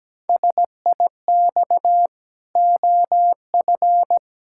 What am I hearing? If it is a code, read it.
Morse code, heard as a keyed tone, about 17 wpm, 700 Hz. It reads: SIX OF
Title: Pangram Challenge v2 - 17wpm